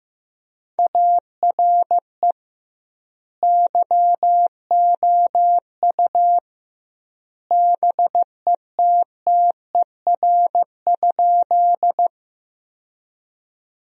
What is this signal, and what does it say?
Morse code, heard as a keyed tone, about 16 wpm, 700 Hz. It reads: ARE YOU BETTER?
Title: 2x Word-Spacing - Sentences from Top 400 Words - 15wpm